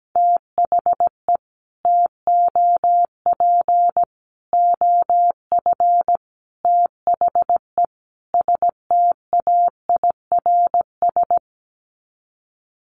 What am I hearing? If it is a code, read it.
Morse code, heard as a keyed tone, about 17 wpm, 700 Hz. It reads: THE TOP OF THE STAIRS
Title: Sets of 5 Words Encore 17wpm